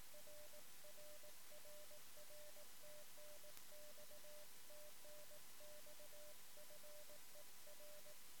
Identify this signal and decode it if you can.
Morse code, heard as a keyed tone, about 18 wpm, 600 Hz. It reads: RRRR TNX TNX FER
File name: sm7lcb / 1 oh2pu-p 13cm